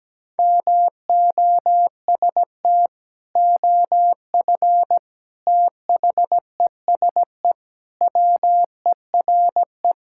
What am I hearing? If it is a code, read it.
Morse code, heard as a keyed tone, about 17 wpm, 700 Hz. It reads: MOST OF THESE WERE
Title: Sentences from Top 1000 Words - Review Part 5 - 17wpm